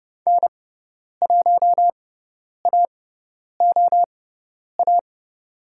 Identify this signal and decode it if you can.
Morse code, heard as a keyed tone, about 30 wpm, spaced as for 10 wpm, 700 Hz. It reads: N1AOA
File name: course / 30:10wpm / 102 Lesson 17g - Letter u - Call Signs 30wpm